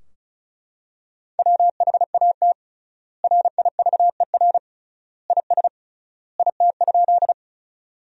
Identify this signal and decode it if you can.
Morse code, heard as a keyed tone, about 36 wpm, 700 Hz. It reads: WHAT RIVER IS IT?
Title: 3x Word-Spacing - Sentences from Top 300 Words - 35wpm